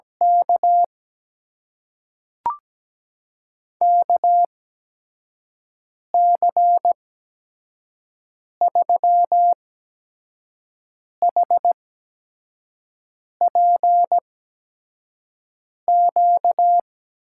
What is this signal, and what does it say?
Morse code, heard as a keyed tone, about 17 wpm, spaced as for 5 wpm, 700 Hz. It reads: K KC3HPQ
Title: USA Call Signs - 6 Characters - 8x Character Spacing - 17wpm